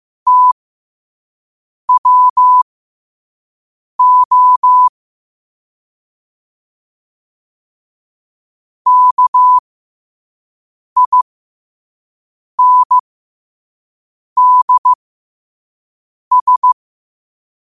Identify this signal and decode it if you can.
Morse code, heard as a keyed tone, about 15 wpm, spaced as for 5 wpm, 1000 Hz. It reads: TWO KINDS
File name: qotdfiles / 5wpm